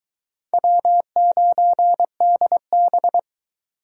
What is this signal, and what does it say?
Morse code, heard as a keyed tone, about 20 wpm, 700 Hz. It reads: W9DB